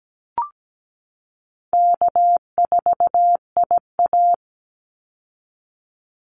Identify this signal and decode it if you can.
Morse code, heard as a keyed tone, about 17 wpm, 700 Hz. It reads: K4IA
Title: Easy Call Signs Repeated 2x - 17wpm